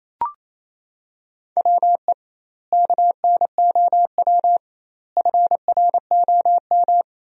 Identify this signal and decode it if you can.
Morse code, heard as a keyed tone, about 27 wpm, 700 Hz. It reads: WE KNOW FROM
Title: 2x Word-Spacing - Sentences from Top 500 Words - 28wpm